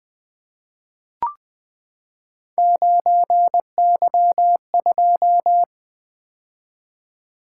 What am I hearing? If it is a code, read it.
Morse code, heard as a keyed tone, about 20 wpm, 700 Hz. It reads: 9Y2
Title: Three Letter-Number Combinations - 20wpm